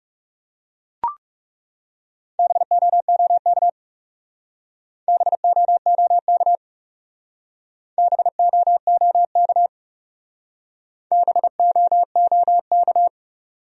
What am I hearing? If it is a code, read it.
Morse code, heard as a keyed tone, about 35 wpm, 700 Hz. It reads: BOOK BOOK BOOK BOOK
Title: Speed-Racing Top 300 Words 30wpm